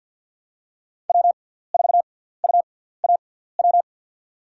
Morse code, heard as a keyed tone, about 53 wpm, spaced as for 17 wpm, 700 Hz. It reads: W4UAW
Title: USA Call Signs - 5 Characters - 6x Character Spacing - 50wpm